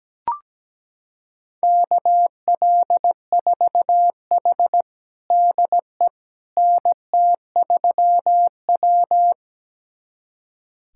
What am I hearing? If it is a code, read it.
Morse code, heard as a keyed tone, about 17 wpm, 700 Hz. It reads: KL4H DE NT3W
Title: USA Callsign DE Callsign - 4 Characters - 17wpm